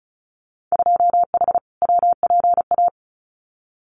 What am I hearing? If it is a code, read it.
Morse code, heard as a keyed tone, about 34 wpm, 700 Hz. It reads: 2H WPA
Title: Winter Field Day Exchanges 35wpm